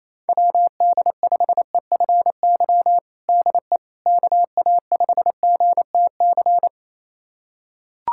Morse code, heard as a keyed tone, about 28 wpm, 700 Hz. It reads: WD5EFY DE KA5GTC
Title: USA Callsign DE Callsign - 6 Characters - 28wpm